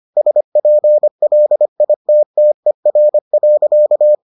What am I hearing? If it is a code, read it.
Morse code, heard as a keyed tone, about 25 wpm, 600 Hz. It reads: SPLITTER.